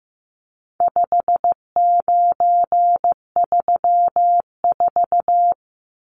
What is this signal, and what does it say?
Morse code, heard as a keyed tone, about 15 wpm, 700 Hz. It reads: 5934
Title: Numbers - 4-Digit 15wpm